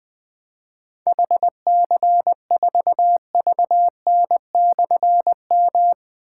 Morse code, heard as a keyed tone, about 21 wpm, 700 Hz. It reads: HC4VN/M
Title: Difficult Call Signs 20wpm